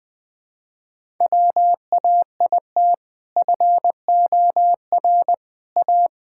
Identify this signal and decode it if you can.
Morse code, heard as a keyed tone, about 20 wpm, 700 Hz. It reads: WAIT FOR A
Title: Sentences with 2-Word Phrases - 20wpm